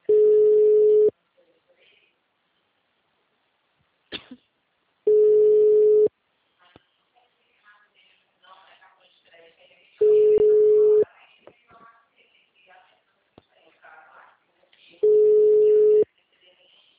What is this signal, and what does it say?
Ringback tone (Europe, 425 Hz), with speech.